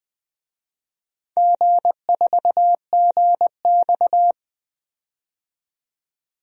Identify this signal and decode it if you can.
Morse code, heard as a keyed tone, about 20 wpm, 700 Hz. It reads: G4GX
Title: DXCC-100 Call Signs - 4 Characters - 20wpm